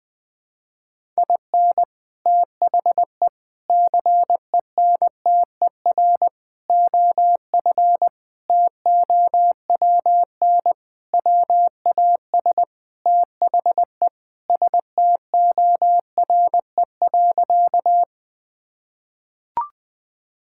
Morse code, heard as a keyed tone, about 20 wpm, 700 Hz. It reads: IN THE CENTER OF TOWN WAS THE STORE.
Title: Sentences with 4-Word Phrases - 20wpm